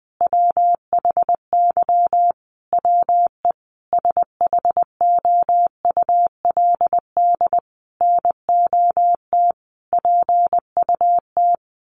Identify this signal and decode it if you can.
Morse code, heard as a keyed tone, about 20 wpm, 700 Hz. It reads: WHY WE SHOULD NOT PUT
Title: Sentences from Top 1100 Words 20wpm